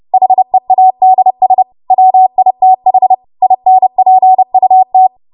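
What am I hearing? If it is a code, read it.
Morse code, heard as a keyed tone, about 30 wpm, 750 Hz. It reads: HEADS WITH INPUT
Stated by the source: W1AW Web Code Practice Files, 30 WPM run from ARRL; text from April 2023 QST.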